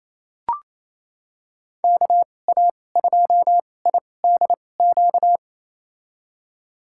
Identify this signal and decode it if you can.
Morse code, heard as a keyed tone, about 27 wpm, spaced as for 20 wpm, 700 Hz. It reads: KA2IDQ
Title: USA Call Signs - 6 Characters - 2x Character Spacing - 28wpm